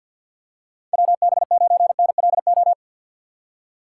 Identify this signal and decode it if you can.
Morse code, heard as a keyed tone, about 53 wpm, 700 Hz. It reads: WB9NLO
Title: USA Call Signs - 6 Characters - 3x Character Spacing - 50wpm